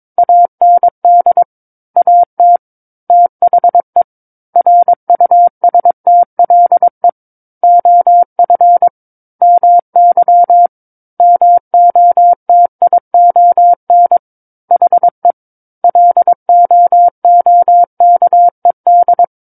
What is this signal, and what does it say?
Morse code, heard as a keyed tone, about 22 wpm, 700 Hz. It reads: AND AT THE RUSTLE OF MY MOTION HE LOOKED
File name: Book One / 13-Chapter 13 - 22 WPM